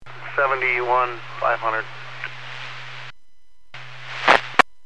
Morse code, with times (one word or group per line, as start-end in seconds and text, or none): none